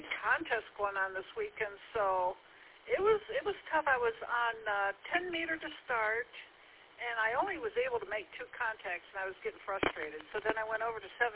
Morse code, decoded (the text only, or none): none